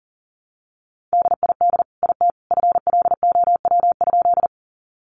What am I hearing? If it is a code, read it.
Morse code, heard as a keyed tone, about 40 wpm, 700 Hz. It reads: DID IT FLOW?
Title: Sentences from Top 1200 Words 40wpm